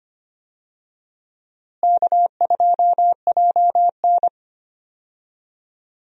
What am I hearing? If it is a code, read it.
Morse code, heard as a keyed tone, about 25 wpm, 700 Hz. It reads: K2JN
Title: USA Call Signs - 4 Characters - Repeated 2x - 25wpm